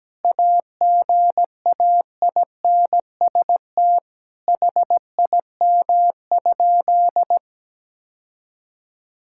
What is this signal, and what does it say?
Morse code, heard as a keyed tone, about 17 wpm, 700 Hz. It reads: AGAINST HIM?
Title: Sentences from Top 1000 Words - Review Part 6 - 17wpm